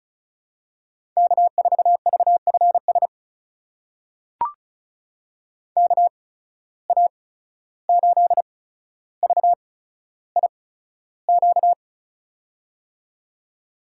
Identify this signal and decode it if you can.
Morse code, heard as a keyed tone, about 36 wpm, spaced as for 14 wpm, 700 Hz. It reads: K4VFS KA8VIQ
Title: USA Call Signs - 8x Character Spacing - 35wpm